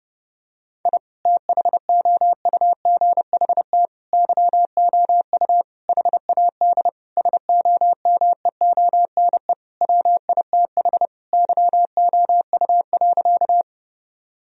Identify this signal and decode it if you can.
Morse code, heard as a keyed tone, about 30 wpm, 700 Hz. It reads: I THOUGHT YOU HAD SOMEONE WITH YOU.
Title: Sentences from Top 1000 Words - Review Part 4 - 30wpm